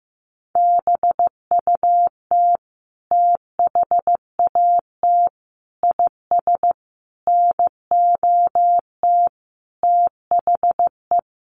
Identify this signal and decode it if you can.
Morse code, heard as a keyed tone, about 15 wpm, 700 Hz. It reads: BUT THAT IS NOT THE
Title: Sentences from Top 600 Words 15wpm